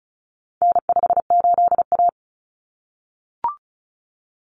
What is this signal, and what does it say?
Morse code, heard as a keyed tone, about 34 wpm, 700 Hz. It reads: N58A
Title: Common Cut Numbers - 4-Digit 35wpm